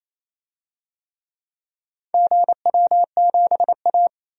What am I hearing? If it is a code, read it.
Morse code, heard as a keyed tone, about 28 wpm, 700 Hz. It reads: GW7A
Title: DXCC-100 Call Signs - 4 Characters - 28wpm